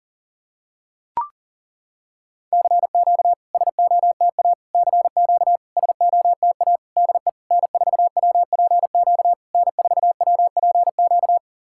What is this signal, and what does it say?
Morse code, heard as a keyed tone, about 40 wpm, 690 Hz. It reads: CQ SOTA CQ SOTA DE N4WPQ N4WPQ
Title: SOTA QSO Practice 40wpm